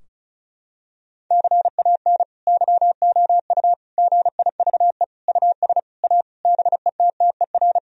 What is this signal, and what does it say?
Morse code, heard as a keyed tone, about 36 wpm, 700 Hz. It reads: CAN YOU GIVE US A BETTER
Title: Sentences from Top 1000 Words - Review Part 4 - 35wpm